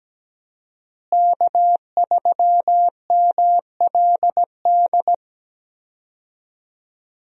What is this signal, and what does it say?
Morse code, heard as a keyed tone, about 17 wpm, 700 Hz. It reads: K3MLD
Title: USA Call Signs - 5 Characters - Repeated 2x - 17wpm